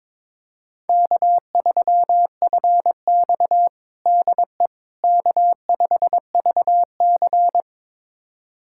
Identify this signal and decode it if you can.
Morse code, heard as a keyed tone, about 22 wpm, 700 Hz. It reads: K3FX DE K5VC
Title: USA Callsign DE Callsign - 4 Characters - 22wpm